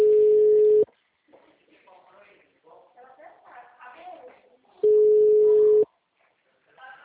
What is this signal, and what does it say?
Ringback tone (Europe, 425 Hz), with speech.